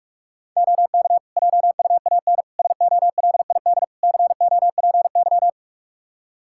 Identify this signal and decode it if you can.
Morse code, heard as a keyed tone, about 41 wpm, 700 Hz. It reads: OK JUAN SOLID COPY
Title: K1USN SST QSO Practice 45wpm